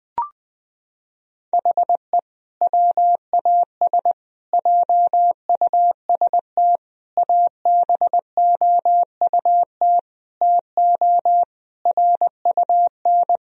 Morse code, heard as a keyed tone, about 20 wpm, 700 Hz. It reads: HE WAS JUST ABOUT TO RUN
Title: Sentences from Top 1000 Words - Review Part 2 - 20wpm